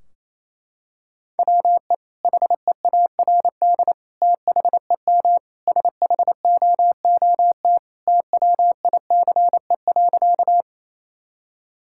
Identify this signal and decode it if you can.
Morse code, heard as a keyed tone, about 28 wpm, 700 Hz. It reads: WE HEARD THEM SHOOT TWICE.
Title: Sentences from Top 1600 Words 28wpm